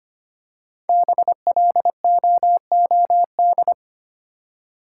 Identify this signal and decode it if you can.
Morse code, heard as a keyed tone, about 25 wpm, 700 Hz. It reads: BLOOD
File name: 5 Letter Words - First 3 Letters Given - 25wpm